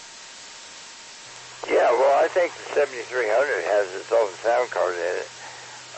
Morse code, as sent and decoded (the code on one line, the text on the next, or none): none